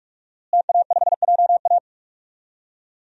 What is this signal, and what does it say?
Morse code, heard as a keyed tone, about 42 wpm, 700 Hz. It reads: TA5JA
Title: DXCC-100 Call Signs - 5 Characters - Repeated 2x - 45wpm